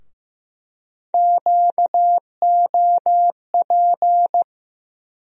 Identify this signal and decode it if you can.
Morse code, heard as a keyed tone, about 15 wpm, 700 Hz. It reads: QOP
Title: USA Call Sign Postfixes - 3 Letters 15wpm